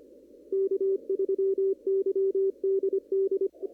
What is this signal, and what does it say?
Morse code, heard as a keyed tone, about 25 wpm, 390 Hz. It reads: K3YDD
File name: CW / kt5j wpx cw-op-k5pi